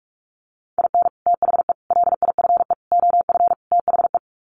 Morse code, heard as a keyed tone, about 44 wpm, 700 Hz. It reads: IN THE LIFE OF THE
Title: Sets of 5 Words Encore2 45wpm